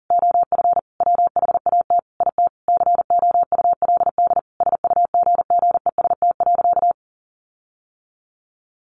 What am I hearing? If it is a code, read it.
Morse code, heard as a keyed tone, about 40 wpm, 700 Hz. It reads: OF WHAT IT COULD SUGGEST.
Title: Sentences from Top 1500 Words 40wpm